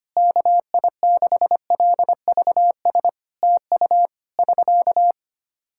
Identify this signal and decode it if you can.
Morse code, heard as a keyed tone, about 25 wpm, 700 Hz. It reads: KI6LVS TU <SK>
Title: Practice QSO - Exchange 4 - Ending - 25wpm